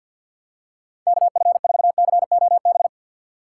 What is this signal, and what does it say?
Morse code, heard as a keyed tone, about 53 wpm, 700 Hz. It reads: KF4COB
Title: USA Call Signs - 6 Characters - 6x Character Spacing - 50wpm